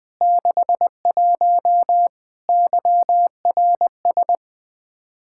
Morse code, heard as a keyed tone, about 20 wpm, 700 Hz. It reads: 61 YRS
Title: K1USN SST QSO Practice 20wpm